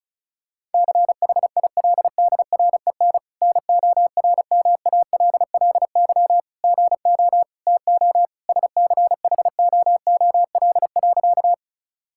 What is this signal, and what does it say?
Morse code, heard as a keyed tone, about 34 wpm, 700 Hz. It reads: CHILDREN NORMALLY GO TO SCHOOL.
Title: Sentences from Top 1000 Words - Review Part 6 - 35wpm